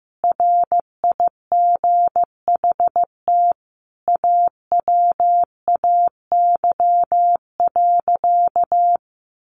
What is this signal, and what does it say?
Morse code, heard as a keyed tone, about 15 wpm, 700 Hz. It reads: RIGHT AWAY.
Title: Sentences from Top 1500 Words 15wpm